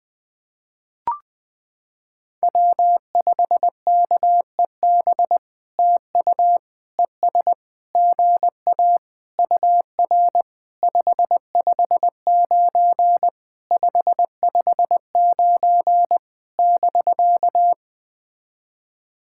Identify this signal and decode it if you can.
Morse code, heard as a keyed tone, about 20 wpm, 700 Hz. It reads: W5KEB TU ES GA UR 559 559 <BK>
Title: POTA QSO Practice - 20wpm